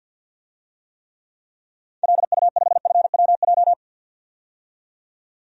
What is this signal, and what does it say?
Morse code, heard as a keyed tone, about 48 wpm, 700 Hz. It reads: LU5FWJ
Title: DXCC-100 Call Signs - 6 Characters - 50wpm